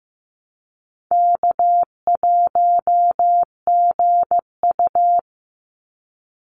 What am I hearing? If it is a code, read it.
Morse code, heard as a keyed tone, about 15 wpm, 700 Hz. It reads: K1GU
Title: Easy Call Signs Repeated 3x - 15wpm